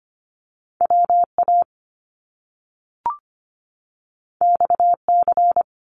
Morse code, heard as a keyed tone, about 25 wpm, 700 Hz. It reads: WA XC
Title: Top Two Letter Combinations in Words 25wpm